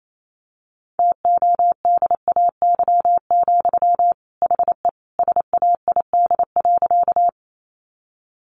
Morse code, heard as a keyed tone, about 28 wpm, 700 Hz. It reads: TODAY, HE SAID.
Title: Sentences from Top 600 Words 28wpm